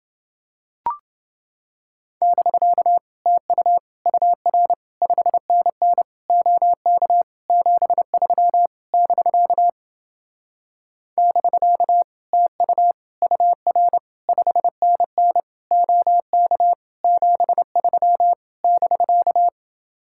Morse code, heard as a keyed tone, about 28 wpm, 700 Hz. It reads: <BK> TU UR 5NN OK 73 <BK> <BK> TU UR 5NN OK 73 <BK>